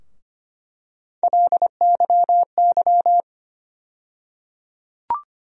Morse code, heard as a keyed tone, about 25 wpm, 700 Hz. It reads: LYY
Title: USA Call Sign Postfixes - 3 Letters 25wpm